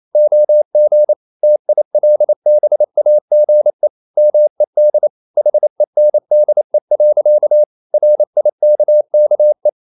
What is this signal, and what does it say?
Morse code, heard as a keyed tone, about 28 wpm, 600 Hz. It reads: OG TILBAGE MED HENDE. RIKKE
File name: Ferie QRQ 28wpm